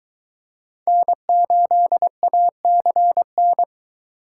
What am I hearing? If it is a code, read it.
Morse code, heard as a keyed tone, about 23 wpm, 700 Hz. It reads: N8ACN